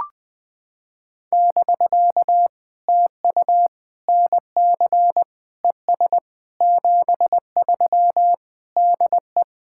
Morse code, heard as a keyed tone, about 20 wpm, 700 Hz. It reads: <BK> TU NC ES 73 DE
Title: POTA QSO Practice - 20wpm